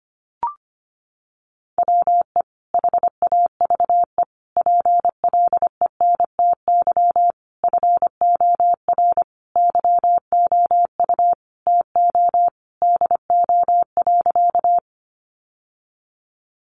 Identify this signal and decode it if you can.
Morse code, heard as a keyed tone, about 25 wpm, 700 Hz. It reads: WE HAVE PLENTY FOR YOU TO DO.
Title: Sentences from Top 1300 Words 25wpm